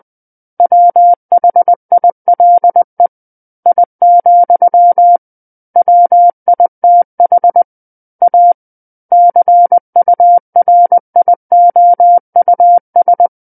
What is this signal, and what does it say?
Morse code, heard as a keyed tone, about 20 wpm, 700 Hz. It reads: WHILE I, WITH A CURIOUS
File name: 20 WPM